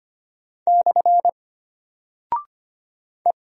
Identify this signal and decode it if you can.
Morse code, heard as a keyed tone, about 25 wpm, 700 Hz. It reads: / E
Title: Single Character (Letter:Number:Punctuation:Prosign) Alt 25wpm